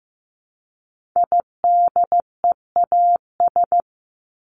Morse code, heard as a keyed tone, about 15 wpm, 700 Hz. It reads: IDEAS
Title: ES:S Suffix - Top 100 Words with Base Words 15wpm